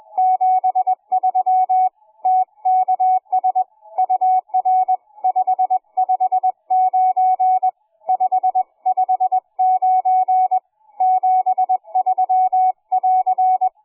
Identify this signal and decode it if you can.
Morse code, heard as a keyed tone, about 20 wpm, 740 Hz. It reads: 73 TKS UR 559 559 73<AR>